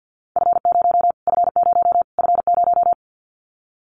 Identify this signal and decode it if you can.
Morse code, heard as a keyed tone, about 50 wpm, 700 Hz. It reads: F0 F0 F0
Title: Callsign Prefixes for Top 25 DXCC - Repeated 3x - 50wpm